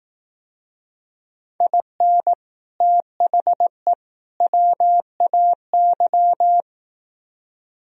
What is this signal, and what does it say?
Morse code, heard as a keyed tone, about 18 wpm, 700 Hz. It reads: IN THE WAY
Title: Speed-Racing - Sets of 3 Words 15wpm